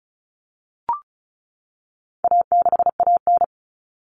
Morse code, heard as a keyed tone, about 34 wpm, 700 Hz. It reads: A6AN
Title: Common Cut Numbers - 4-Digit 35wpm